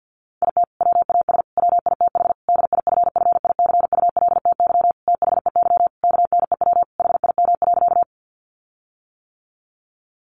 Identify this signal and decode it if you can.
Morse code, heard as a keyed tone, about 50 wpm, 700 Hz. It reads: IT WAS WITH DIFFICULTY THEY KNEW HIM.
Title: Sentences from Top 1500 Words 50wpm